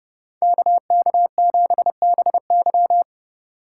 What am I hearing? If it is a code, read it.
Morse code, heard as a keyed tone, about 27 wpm, 700 Hz. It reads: KK7BY